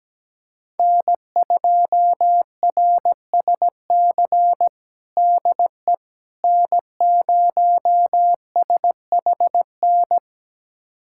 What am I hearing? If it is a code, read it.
Morse code, heard as a keyed tone, about 17 wpm, 700 Hz. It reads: N2RSC DE N0SHN
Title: USA Callsign DE Callsign - 5 Characters - 17wpm